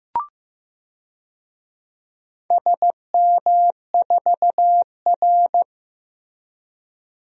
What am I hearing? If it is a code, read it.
Morse code, heard as a keyed tone, about 15 wpm, 700 Hz. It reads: SM4R
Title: DXCC-100 Call Signs - 4 Characters - Repeated 2x - 15wpm